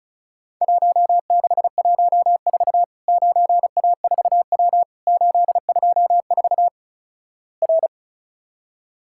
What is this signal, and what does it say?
Morse code, heard as a keyed tone, about 36 wpm, 700 Hz. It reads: 1614 9A4W 824 R
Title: Worked All Europe DX Contest - QTC Exchange - 35wpm